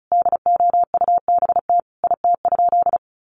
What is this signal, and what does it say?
Morse code, heard as a keyed tone, about 34 wpm, 700 Hz. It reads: DOUBT IT?
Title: Sentences from Top 1500 Words 35wpm